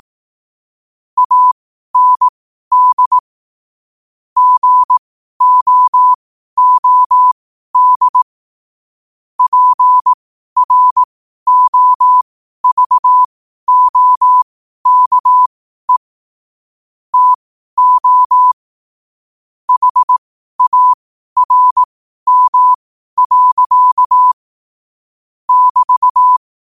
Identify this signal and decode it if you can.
Morse code, heard as a keyed tone, about 18 wpm, spaced as for 12 wpm, 1000 Hz. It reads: AND GOOD PROVOKE TO HARM. <BT>
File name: qotdfiles / 12wpm